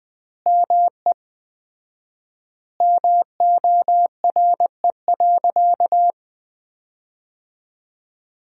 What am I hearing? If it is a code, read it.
Morse code, heard as a keyed tone, about 20 wpm, 700 Hz. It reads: ME MORE.
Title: 4x Word-Spacing - Sentences from Top 300 Words - 20wpm